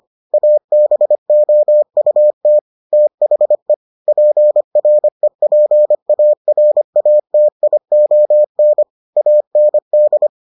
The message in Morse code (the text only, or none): ABOUT THE PREPARATION AND